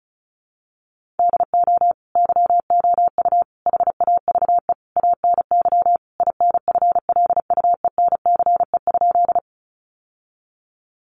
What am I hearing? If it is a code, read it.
Morse code, heard as a keyed tone, about 34 wpm, 700 Hz. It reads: DO YOU HAVE ANY INFLUENCE?
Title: Sentences from Top 1100 Words 35wpm